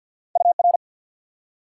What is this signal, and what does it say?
Morse code, heard as a keyed tone, about 54 wpm, 700 Hz. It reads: UR
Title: Top 90 QSO Elements 50wpm